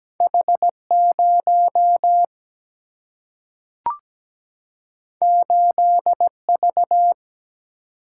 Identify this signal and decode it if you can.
Morse code, heard as a keyed tone, about 17 wpm, 700 Hz. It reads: H 0 8 V
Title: Single-Letter Single-Number 17wpm